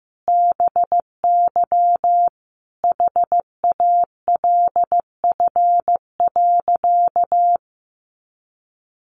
Morse code, heard as a keyed tone, about 15 wpm, 700 Hz. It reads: BY HALF.
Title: Sentences from Top 1100 Words 15wpm